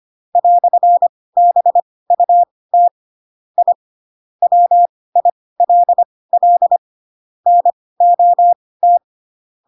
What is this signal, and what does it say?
Morse code, heard as a keyed tone, about 25 wpm, spaced as for 18 wpm, 700 Hz. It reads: "BUT I WILL NOT
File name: Pinocchio / Pinocchio ch7 12 25 ik1hge